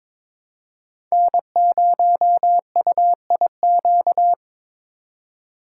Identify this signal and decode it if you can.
Morse code, heard as a keyed tone, about 22 wpm, 700 Hz. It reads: N0UIQ
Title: USA Call Signs - 5 Characters - 3x Character Spacing - 22wpm